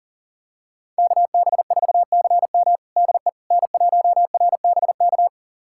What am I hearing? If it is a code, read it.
Morse code, heard as a keyed tone, about 40 wpm, 700 Hz. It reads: KB4CM DE N1RBK
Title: USA Callsign DE Callsign - 5 Characters - 40wpm